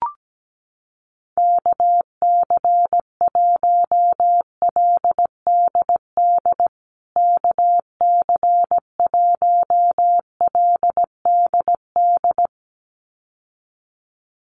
Morse code, heard as a keyed tone, about 17 wpm, 700 Hz. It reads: KC1LDD KC1LDD
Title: USA Call Signs Repeated 2x 17wpm